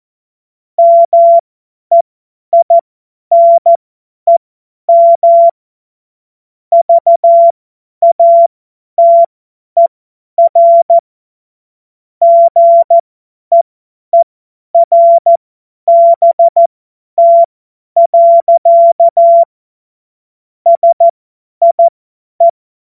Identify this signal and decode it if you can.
Morse code, heard as a keyed tone, about 14 wpm, spaced as for 10 wpm, 680 Hz. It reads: MEINEM VATER GEERBT. SIE
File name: Morsekurs / Witze / Witz 27. Januar 2026 50 F70